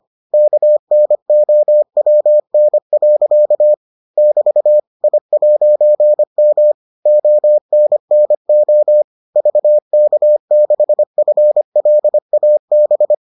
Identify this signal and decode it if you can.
Morse code, heard as a keyed tone, about 25 wpm, 600 Hz. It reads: KNOWN. <BT> I'M ONNO VK6FLAB